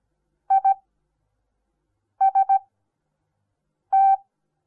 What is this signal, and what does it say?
Morse code, heard as a keyed tone, about 17 wpm, spaced as for 5 wpm, 780 Hz. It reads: IST
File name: Onlinelehrgang / morsen / DJ4UF Morsen08